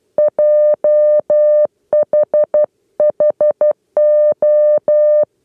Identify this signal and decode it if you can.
Morse code, heard as a keyed tone, about 12 wpm, 600 Hz. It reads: JHHO
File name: 8 lezione cw 2